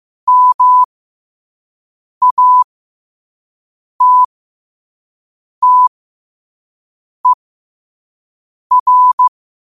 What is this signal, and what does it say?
Morse code, heard as a keyed tone, about 15 wpm, spaced as for 5 wpm, 1000 Hz. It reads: MATTER
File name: qotdfiles / 5wpm